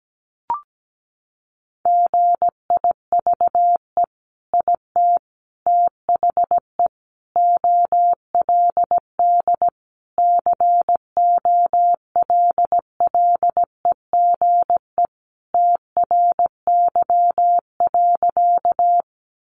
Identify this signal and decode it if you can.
Morse code, heard as a keyed tone, about 17 wpm, 700 Hz. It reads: GIVE IT THE OLD COLLEGE TRY.
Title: Sentences from Top 1300 Words 17wpm